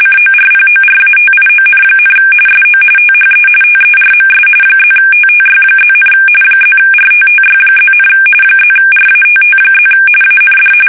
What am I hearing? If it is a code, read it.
RTTY, about 45.45 baud. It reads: CORPORATION, SO THE TERMS +TTY+, +RTTY+, +RATT+ AND
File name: fsk850 / teletype-RTTY